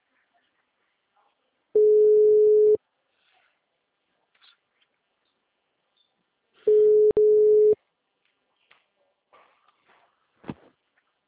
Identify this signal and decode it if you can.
Ringback tone (Europe, 425 Hz).